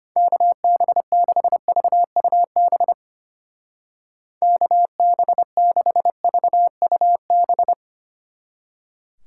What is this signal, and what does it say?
Morse code, heard as a keyed tone, about 31 wpm, 700 Hz. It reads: KB6VUB KB6VUB
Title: Speed Racing - USA 2x3 Call Signs 25wpm